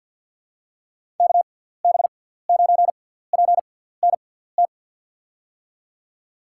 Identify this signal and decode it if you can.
Morse code, heard as a keyed tone, about 52 wpm, spaced as for 17 wpm, 700 Hz. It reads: KB9PNT
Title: USA Call Signs - 6 Characters - 6x Character Spacing - 50wpm